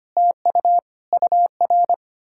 Morse code, heard as a keyed tone, about 33 wpm, 700 Hz. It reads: TU UR